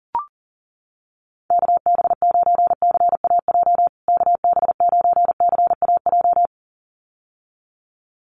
Morse code, heard as a keyed tone, about 40 wpm, 700 Hz. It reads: KB9CAJ KB9CAJ
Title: USA Call Signs Repeated 2x 40wpm